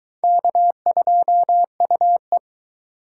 Morse code, heard as a keyed tone, about 23 wpm, 700 Hz. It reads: K2UE